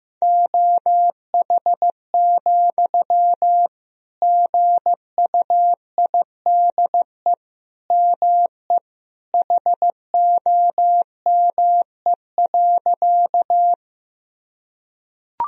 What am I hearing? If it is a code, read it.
Morse code, heard as a keyed tone, about 15 wpm, 700 Hz. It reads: OH, GUIDE ME HOME.
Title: Sentences from Top 1000 Words - Review Part 3 - 15wpm